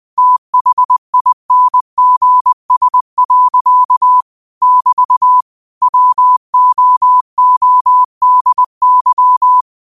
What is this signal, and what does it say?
Morse code, heard as a keyed tone, about 20 wpm, 1000 Hz. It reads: THINGS. <BT> WOODY
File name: qotdfiles / 20wpm